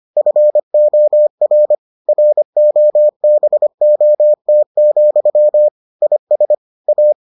Morse code, heard as a keyed tone, about 25 wpm, 600 Hz. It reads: FOR ROBOT, IS A